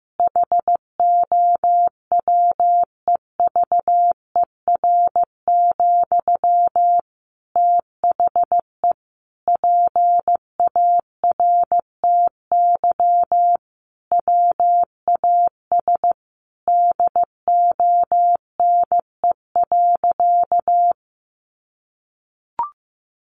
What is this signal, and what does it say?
Morse code, heard as a keyed tone, about 15 wpm, 700 Hz. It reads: HOWEVER, THE PARTY WAS DONE.
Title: Sentences from Top 1400 Words 15wpm